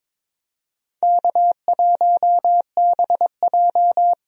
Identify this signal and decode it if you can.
Morse code, heard as a keyed tone, about 21 wpm, 700 Hz. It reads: K1BJ
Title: USA Call Signs - 4 Characters - 22wpm